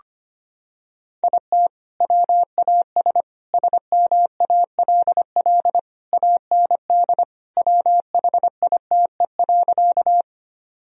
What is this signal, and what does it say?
Morse code, heard as a keyed tone, about 25 wpm, 700 Hz. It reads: IT WAS SMALL AND WHITE.
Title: Sentences from Top 1000 Words - Review Part 1 - 25wpm